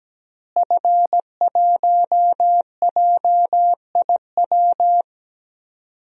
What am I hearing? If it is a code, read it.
Morse code, heard as a keyed tone, about 17 wpm, 700 Hz. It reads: F1JIW
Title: DXCC-100 Call Signs - 5 Characters - Repeated 3x - 17wpm